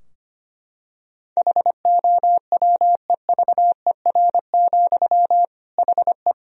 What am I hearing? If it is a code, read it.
Morse code, heard as a keyed tone, about 25 wpm, 700 Hz. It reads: HOWEVER, HE
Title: Sentences from Top 1000 Words - Review Part 3 - 25wpm